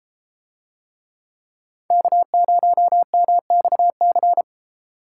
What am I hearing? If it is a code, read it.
Morse code, heard as a keyed tone, about 31 wpm, 700 Hz. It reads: K0MXC